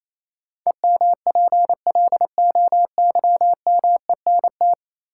Morse code, heard as a keyed tone, about 28 wpm, 700 Hz. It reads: EMPLOYMENT
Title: Sentences from Top 1500 Words - New Word Drill - 28wpm